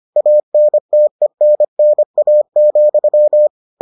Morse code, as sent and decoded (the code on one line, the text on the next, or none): .- -. - . -. -. .- --..--
ANTENNA,